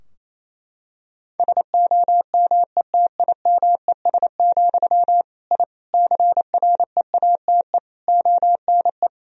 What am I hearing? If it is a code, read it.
Morse code, heard as a keyed tone, about 28 wpm, 700 Hz. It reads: SOMETIMES, I CREATE ONE
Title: Sentences from Top 1000 Words - Review Part 3 - 28wpm